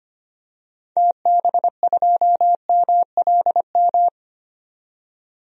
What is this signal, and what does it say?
Morse code, heard as a keyed tone, about 25 wpm, 700 Hz. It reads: TB2MLM
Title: DXCC-100 Call Signs - 6 Characters - 25wpm